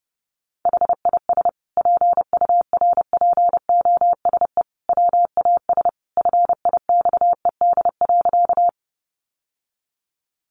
Morse code, heard as a keyed tone, about 30 wpm, 700 Hz. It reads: HIS PURPOSE WAS FIXED.
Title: Sentences from Top 1100 Words 30wpm